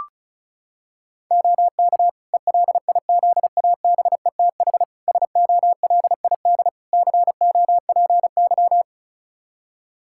Morse code, heard as a keyed tone, about 36 wpm, 700 Hz. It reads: OK ELIZABETH SOLID COPY
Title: Practice QSO - Exchange 3 - More Info - 35wpm